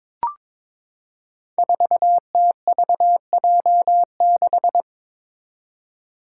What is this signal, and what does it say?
Morse code, heard as a keyed tone, about 22 wpm, 700 Hz. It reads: 4TVJ6
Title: Five Challenging Letter-Number Combinations 22wpm